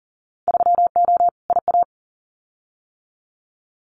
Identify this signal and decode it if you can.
Morse code, heard as a keyed tone, about 41 wpm, 700 Hz. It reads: 3O IA
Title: Winter Field Day Exchanges 40wpm